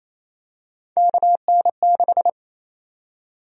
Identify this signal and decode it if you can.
Morse code, heard as a keyed tone, about 28 wpm, 700 Hz. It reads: KN6
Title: USA Call Sign Prefixes - Repeated 2x - 28wpm